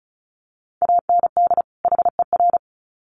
Morse code, heard as a keyed tone, about 34 wpm, 700 Hz. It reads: AND HER
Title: Sets of 2 Words 35wpm